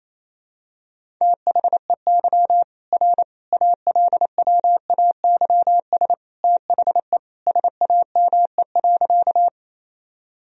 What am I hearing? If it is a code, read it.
Morse code, heard as a keyed tone, about 28 wpm, 700 Hz. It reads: THEY R ALWAYS THE SAME.
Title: Sentences from Top 300 Words and Abbreviations 28wpm